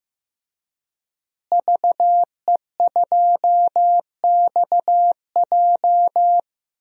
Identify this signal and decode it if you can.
Morse code, heard as a keyed tone, about 15 wpm, 700 Hz. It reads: VE2XJ
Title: DXCC-100 Call Signs - 5 Characters - Repeated 3x - 15wpm